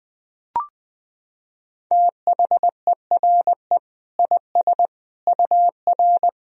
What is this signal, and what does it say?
Morse code, heard as a keyed tone, about 20 wpm, 700 Hz. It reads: THERE IS UR
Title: Sentences from Top 500 Words and Abbreviations 20wpm